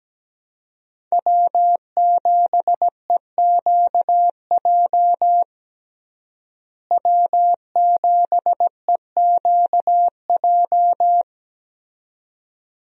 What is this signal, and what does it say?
Morse code, heard as a keyed tone, about 17 wpm, 700 Hz. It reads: W7EQJ W7EQJ
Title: USA Call Signs - 5 Characters - Repeated 2x - 17wpm